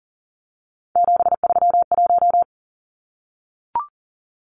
Morse code, heard as a keyed tone, about 40 wpm, 700 Hz. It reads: 731
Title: Numbers - 3-Digit 40wpm